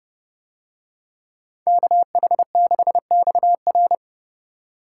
Morse code, heard as a keyed tone, about 29 wpm, 700 Hz. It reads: KH6XR